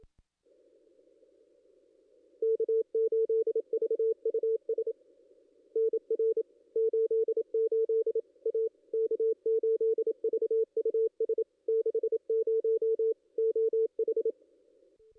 Morse code, heard as a keyed tone, about 28 wpm, 450 Hz. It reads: K8VUS NR 88 A K8VUS 60 OH
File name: contests / ss-cw-k5tr-op-n5rz -1700